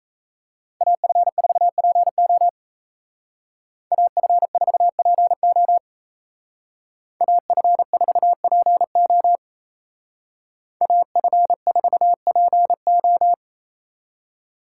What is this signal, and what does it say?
Morse code, heard as a keyed tone, about 33 wpm, 700 Hz. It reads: AF4PO AF4PO AF4PO AF4PO
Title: Speed Racing - USA Call Signs 28wpm